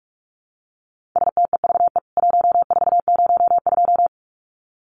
Morse code, heard as a keyed tone, about 44 wpm, 700 Hz. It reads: STEVE 1402
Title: CWT Exchange 45wpm